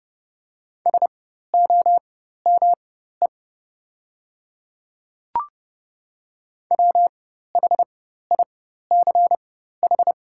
Morse code, heard as a keyed tone, about 30 wpm, spaced as for 14 wpm, 700 Hz. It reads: SOME WHICH
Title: 4x Character Spacing - Top 100 Words - 30wpm